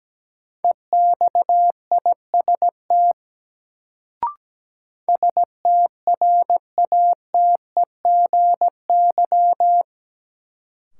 Morse code, heard as a keyed tone, about 17 wpm, 700 Hz. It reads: EXIST STRATEGY
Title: Sentences from Top 1000 Words - Review Part 5 - New Word Drill - 17wpm